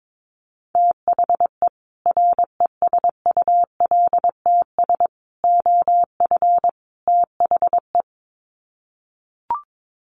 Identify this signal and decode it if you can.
Morse code, heard as a keyed tone, about 22 wpm, 700 Hz. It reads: THE RESULTS OF THE
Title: Sets of 4 Words 22wpm